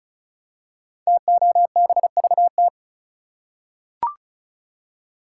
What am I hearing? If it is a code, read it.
Morse code, heard as a keyed tone, about 36 wpm, 700 Hz. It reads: TOBVT
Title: Code Groups - 5 Letters - 35wpm